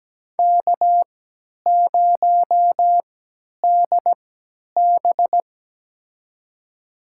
Morse code, heard as a keyed tone, about 17 wpm, spaced as for 10 wpm, 700 Hz. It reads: K0DB
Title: USA Call Signs - 4 Characters - 3x Character Spacing - 17wpm